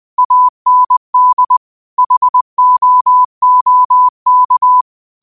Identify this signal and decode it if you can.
Morse code, heard as a keyed tone, about 20 wpm, 1000 Hz. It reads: AND HOOK